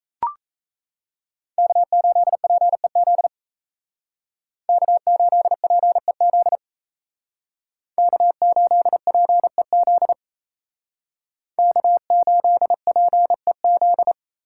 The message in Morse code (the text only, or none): K8PEZ K8PEZ K8PEZ K8PEZ